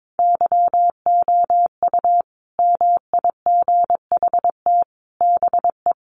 Morse code, heard as a keyed tone, about 22 wpm, 700 Hz. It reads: YOU MIGHT BE
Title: Sets of 5 Words Encore2 22wpm